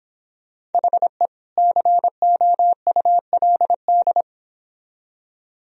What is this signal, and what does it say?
Morse code, heard as a keyed tone, about 29 wpm, 700 Hz. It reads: HE COULD